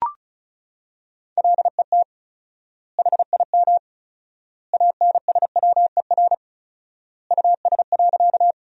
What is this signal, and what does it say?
Morse code, heard as a keyed tone, about 34 wpm, 700 Hz. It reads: LET HIM ANSWER US.
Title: 4x Word-Spacing - Sentences from Top 300 Words - 35wpm